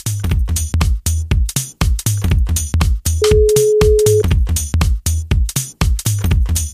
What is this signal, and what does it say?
Ringback tone (Europe, 425 Hz).